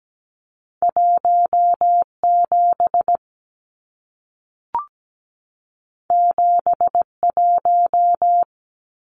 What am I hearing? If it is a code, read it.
Morse code, heard as a keyed tone, about 17 wpm, 700 Hz. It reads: 17 71
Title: Numbers - 2-Digit 17wpm